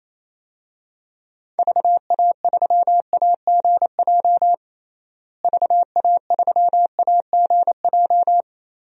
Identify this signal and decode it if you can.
Morse code, heard as a keyed tone, about 28 wpm, 700 Hz. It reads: VA3AGJ VA3AGJ
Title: DXCC-100 Call Signs - 6 Characters - Repeated 3x - 28wpm